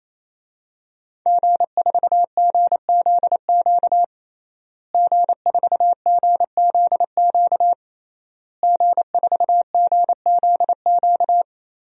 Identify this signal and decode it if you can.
Morse code, heard as a keyed tone, about 28 wpm, 700 Hz. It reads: G4GZQ G4GZQ G4GZQ
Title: DXCC-100 Call Signs - 5 Characters - Repeated 3x - 28wpm